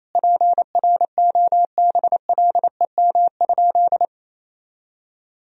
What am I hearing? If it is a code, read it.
Morse code, heard as a keyed tone, about 28 wpm, 700 Hz. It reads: PROBLEM?
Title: Sentences from Top 1000 Words - Review Part 3 - 28wpm